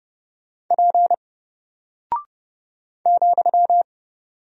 Morse code, heard as a keyed tone, about 28 wpm, 700 Hz. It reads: P ,
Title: Single Character (Letter:Number:Punctuation:Prosign) 30wpm